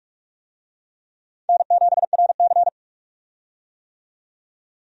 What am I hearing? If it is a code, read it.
Morse code, heard as a keyed tone, about 46 wpm, 700 Hz. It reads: N7RC
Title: USA Call Signs - 4 Characters - 45wpm